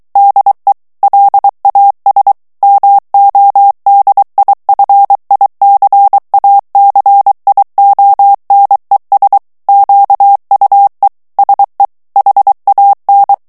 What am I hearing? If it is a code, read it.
Morse code, heard as a keyed tone, about 23 wpm, 800 Hz. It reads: DE LAS MODIFICACIONES QUE SE HAN
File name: morse mp3 / ESPAGNOL TEXTE EN CLAIR 2 VITESSE 22 MOTS MINUTES